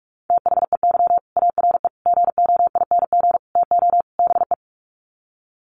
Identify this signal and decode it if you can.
Morse code, heard as a keyed tone, about 44 wpm, 690 Hz. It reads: THEY ARE GOING TO BE
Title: Sets of 5 Words Encore2 45wpm